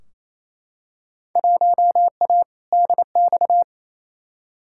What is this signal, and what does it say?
Morse code, heard as a keyed tone, about 28 wpm, 700 Hz. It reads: 1A DX
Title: Field Day Exchanges Encore - 28wpm